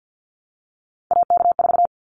Morse code, heard as a keyed tone, about 50 wpm, 700 Hz. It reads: AK4
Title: Callsign Prefixes for Top 25 DXCC - Repeated 3x - 50wpm